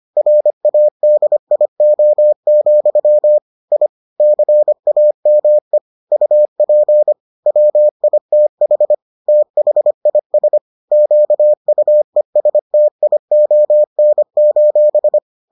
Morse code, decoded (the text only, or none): RADIO, I CAME UP WITH THIS QUESTION: